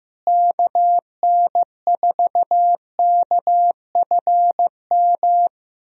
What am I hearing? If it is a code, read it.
Morse code, heard as a keyed tone, about 15 wpm, 700 Hz. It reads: KN4KFM
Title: USA Call Signs - 6 Characters - Repeated 3x - 15wpm